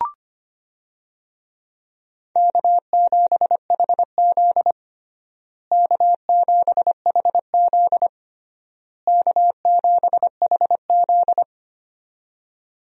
Morse code, heard as a keyed tone, about 25 wpm, 700 Hz. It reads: K7HZ K7HZ K7HZ
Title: USA Call Signs - 4 Characters - Repeated 3x - 25wpm